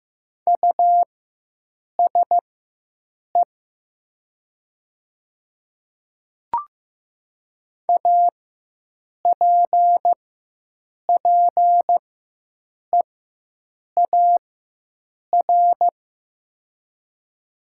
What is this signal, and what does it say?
Morse code, heard as a keyed tone, about 15 wpm, spaced as for 7 wpm, 700 Hz. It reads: USE APPEAR
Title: 4x Character Spacing - Top 500 Words - 15wpm